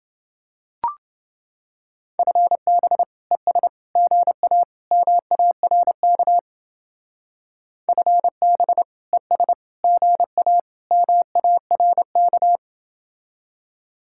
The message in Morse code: FB ES GA MARK FB ES GA MARK

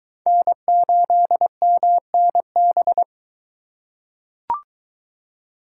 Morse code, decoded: N8MNB

